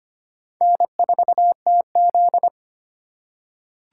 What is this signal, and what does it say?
Morse code, heard as a keyed tone, about 25 wpm, 700 Hz. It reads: N4TZ
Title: K1USN SST QSO Practice - 25wpm